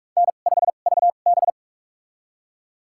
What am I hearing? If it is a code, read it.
Morse code, heard as a keyed tone, about 46 wpm, spaced as for 33 wpm, 700 Hz. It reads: N5VB
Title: USA Call Signs - 4 Characters - 2x Character Spacing - 45wpm